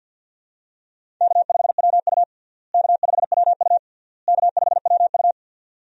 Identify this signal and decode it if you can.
Morse code, heard as a keyed tone, about 54 wpm, 700 Hz. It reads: K5WU K5WU K5WU
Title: USA Call Signs - 4 Characters - Repeated 3x - 50wpm